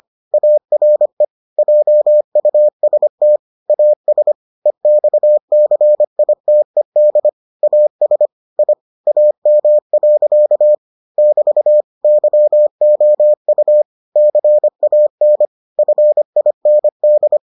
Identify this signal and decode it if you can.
Morse code, heard as a keyed tone, about 25 wpm, 600 Hz. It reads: ARE JUST AS EXCITED AS I AM. <BT> YOU CAN FIND